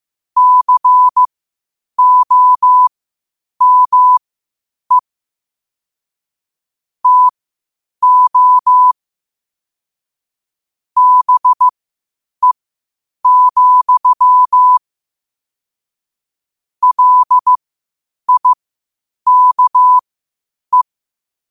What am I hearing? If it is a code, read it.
Morse code, heard as a keyed tone, about 15 wpm, spaced as for 8 wpm, 1000 Hz. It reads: COME TO BE, LIKE
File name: qotdfiles / 8wpm